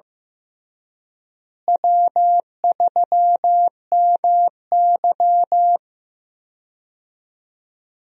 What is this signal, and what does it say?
Morse code, heard as a keyed tone, about 15 wpm, 700 Hz. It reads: W3MY
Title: USA Call Signs - 4 Characters - Repeated 3x - 15wpm